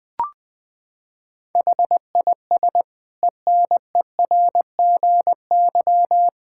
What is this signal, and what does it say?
Morse code, heard as a keyed tone, about 20 wpm, 700 Hz. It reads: HIS ENERGY
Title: Sentences from Top 1000 Words - Review Part 6 - 20wpm